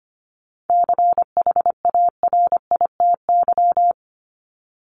Morse code, heard as a keyed tone, about 25 wpm, 700 Hz. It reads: CHARITY
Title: Top 1501-1750 Words 25wpm